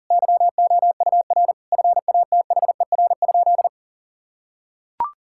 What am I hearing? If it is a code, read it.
Morse code, heard as a keyed tone, about 38 wpm, 700 Hz. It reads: YOUR FATHER?
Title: Sentences from Top 1600 Words 40wpm